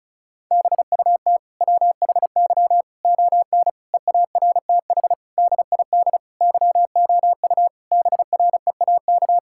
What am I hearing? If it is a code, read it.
Morse code, heard as a keyed tone, about 34 wpm, 700 Hz. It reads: BUT WHY ON EARTH DID YOU BREAK
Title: Sentences from Top 1000 Words - Review Part 6 - 35wpm